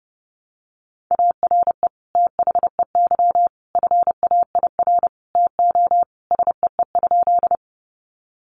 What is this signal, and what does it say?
Morse code, heard as a keyed tone, about 28 wpm, 700 Hz. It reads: ARE THEY FAIR TO SEE?
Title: Sentences from Top 1300 Words 30wpm